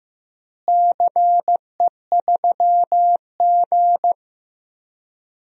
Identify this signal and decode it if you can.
Morse code, heard as a keyed tone, about 15 wpm, 700 Hz. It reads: CE3G
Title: DXCC-100 Call Signs - 4 Characters - Repeated 2x - 15wpm